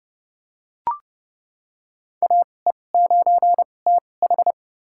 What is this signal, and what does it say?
Morse code, heard as a keyed tone, about 30 wpm, spaced as for 22 wpm, 700 Hz. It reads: AE9TH
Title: USA Call Signs - 2x Character Spacing - 30wpm